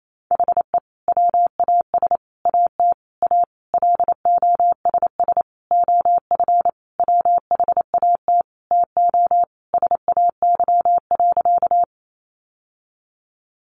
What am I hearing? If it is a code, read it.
Morse code, heard as a keyed tone, about 28 wpm, 700 Hz. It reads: HE WAS AT A LOSS OF WHAT TO SAY.
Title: Sentences from Top 1400 Words 28wpm